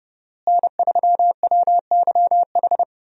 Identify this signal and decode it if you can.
Morse code, heard as a keyed tone, about 28 wpm, 700 Hz. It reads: N3WYH
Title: USA Call Signs - 5 Characters - Repeated 3x - 30wpm